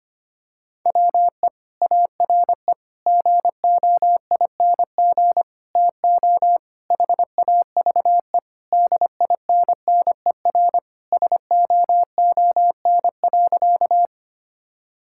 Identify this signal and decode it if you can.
Morse code, heard as a keyed tone, about 25 wpm, 700 Hz. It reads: WE ARE GOING TO HAVE DINNER SOON.
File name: practice / 25wpm / Sentences with 3-Word Phrases - 25wpm